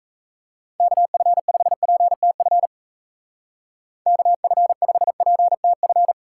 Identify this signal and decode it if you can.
Morse code, heard as a keyed tone, about 34 wpm, 700 Hz. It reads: KF5PTF KF5PTF
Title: Speed Racing - USA 2x3 Call Signs 28wpm